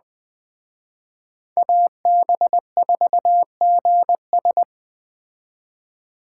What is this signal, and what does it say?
Morse code, heard as a keyed tone, about 20 wpm, 700 Hz. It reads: AB4GS
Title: Speed Racing - USA 2x2 Call Signs 17wpm